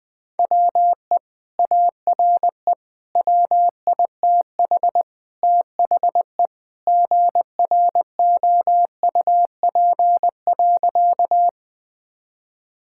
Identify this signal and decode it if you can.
Morse code, heard as a keyed tone, about 20 wpm, 700 Hz. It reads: WE ARE WITH THE GROUP.
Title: Sentences with 2-Word Phrases - 20wpm